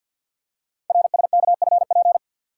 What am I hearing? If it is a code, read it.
Morse code, heard as a keyed tone, about 50 wpm, 700 Hz. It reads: RSKFP
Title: Code Groups - 5 Letters - 50wpm